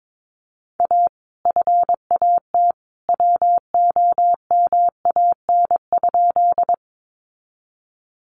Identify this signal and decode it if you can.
Morse code, heard as a keyed tone, about 22 wpm, 700 Hz. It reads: A FAT WOMAN?
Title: Sentences from Top 1100 Words 22wpm